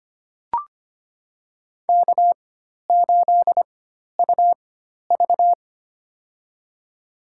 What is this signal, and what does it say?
Morse code, heard as a keyed tone, about 25 wpm, spaced as for 12 wpm, 700 Hz. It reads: K8UV
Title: USA Call Signs - 4 Characters - 4x Character Spacing - 25wpm